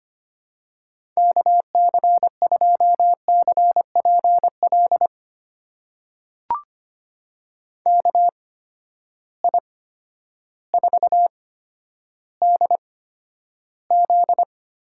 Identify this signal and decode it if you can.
Morse code, heard as a keyed tone, about 25 wpm, spaced as for 11 wpm, 700 Hz. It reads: KC2CPL KI4DZ
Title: USA Call Signs - 8x Character Spacing - 25wpm